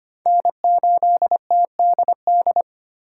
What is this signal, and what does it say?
Morse code, heard as a keyed tone, about 34 wpm, spaced as for 30 wpm, 700 Hz. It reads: N8TDD
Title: Speed Racing - USA 1x3 Call Signs 25wpm